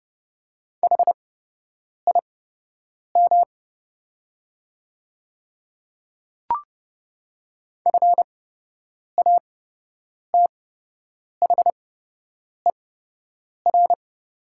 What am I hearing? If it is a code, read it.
Morse code, heard as a keyed tone, about 30 wpm, spaced as for 8 wpm, 700 Hz. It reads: HIM FATHER
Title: 8x Character Spacing - Top 200 Words - 30wpm